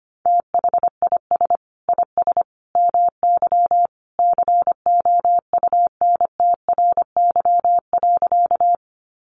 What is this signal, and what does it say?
Morse code, heard as a keyed tone, about 25 wpm, 700 Hz. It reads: THIS IS MY COUNTRY.
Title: Sentences from Top 300 Words 25wpm